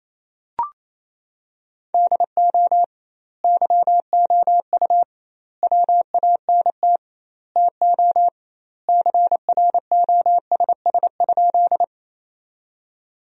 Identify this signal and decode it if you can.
Morse code, heard as a keyed tone, about 28 wpm, 700 Hz. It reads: DO YOU WANT TO CROSS?
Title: 2x Word-Spacing - Sentences from Top 300 Words - 28wpm